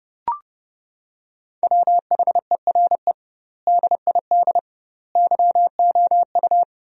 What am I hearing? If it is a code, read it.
Morse code, heard as a keyed tone, about 30 wpm, 700 Hz. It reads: WHERE DID YOU
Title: 2x Word-Spacing - Sentences from Top 200 Words - 30wpm